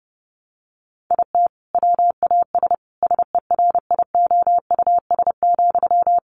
Morse code, heard as a keyed tone, about 30 wpm, 700 Hz. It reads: IT WAS SERIOUS,
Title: Sentences from Top 1100 Words 30wpm